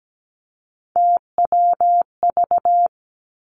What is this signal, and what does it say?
Morse code, heard as a keyed tone, about 17 wpm, 700 Hz. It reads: TWV
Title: Three Challenging Letter-Number Combinations 17wpm